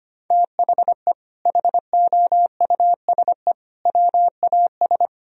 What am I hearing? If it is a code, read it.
Morse code, heard as a keyed tone, about 25 wpm, 700 Hz. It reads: THE HOUSE WAS
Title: Sentences from Top 1000 Words - Review Part 2 - 25wpm